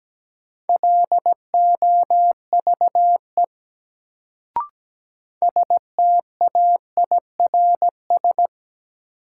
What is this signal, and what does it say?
Morse code, heard as a keyed tone, about 17 wpm, 700 Hz. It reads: LOVE STAIRS
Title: Sets of 5 Words Encore - New Word Drill - 17wpm